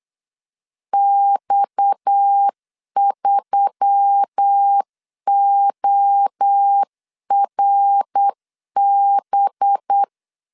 Morse code, heard as a keyed tone, about 8 wpm, 800 Hz. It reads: X3ORB
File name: CW / DET / 1 6 1500C S C 8wpm 5219